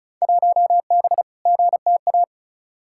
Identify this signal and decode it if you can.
Morse code, heard as a keyed tone, about 36 wpm, 700 Hz. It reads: 1B GTA
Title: Field Day Exchanges Encore - 35wpm